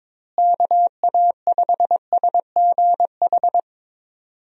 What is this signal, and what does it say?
Morse code, heard as a keyed tone, about 22 wpm, 700 Hz. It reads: KA5SGH
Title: USA Call Signs - 6 Characters - Repeated 2x - 22wpm